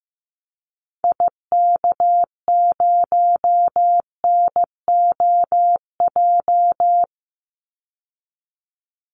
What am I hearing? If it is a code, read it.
Morse code, heard as a keyed tone, about 15 wpm, 700 Hz. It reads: IK0NOJ
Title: Call Signs Repeated 2x - 15wpm